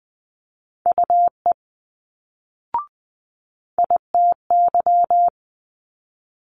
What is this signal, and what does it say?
Morse code, heard as a keyed tone, about 20 wpm, 700 Hz. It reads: UE ITY
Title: Multi-letter Phonemes Quick Fire 20wpm